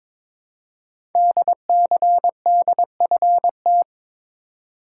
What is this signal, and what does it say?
Morse code, heard as a keyed tone, about 22 wpm, 700 Hz. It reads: DCDFT
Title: Code Groups - 5 Letters - 22wpm